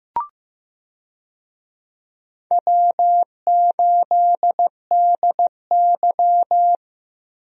Morse code, heard as a keyed tone, about 15 wpm, 700 Hz. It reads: W8DY
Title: USA Call Signs - 4 Characters - Repeated 2x - 15wpm